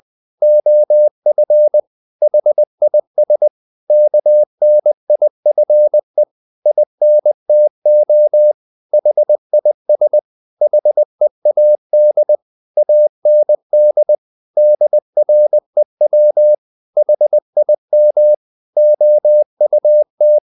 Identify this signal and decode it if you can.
Morse code, heard as a keyed tone, about 20 wpm, 600 Hz. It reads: OF HIS KNIFE INTO HIS HEAD AND DREW HIM OUT
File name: the old man and the sea part 2 20wpm